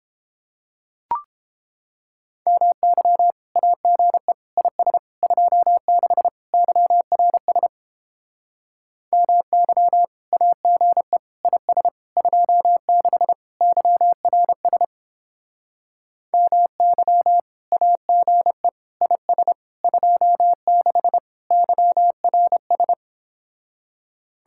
Morse code, heard as a keyed tone, about 29 wpm, 700 Hz. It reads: MY AGE IS 26 YRS MY AGE IS 26 YRS MY AGE IS 26 YRS